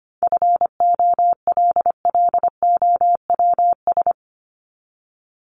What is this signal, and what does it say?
Morse code, heard as a keyed tone, about 25 wpm, 700 Hz. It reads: FOLLOWS
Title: ES:S Suffix - Top 100 Words with Base Words 25wpm